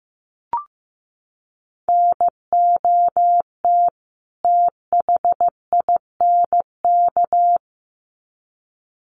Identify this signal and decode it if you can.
Morse code, heard as a keyed tone, about 15 wpm, 700 Hz. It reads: NOT THINK
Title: Sets of 2 Words 15wpm